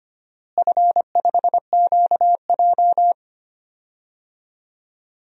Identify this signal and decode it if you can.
Morse code, heard as a keyed tone, about 25 wpm, 700 Hz. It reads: F5QJ
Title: DXCC-100 Call Signs - 4 Characters - Repeated 3x - 25wpm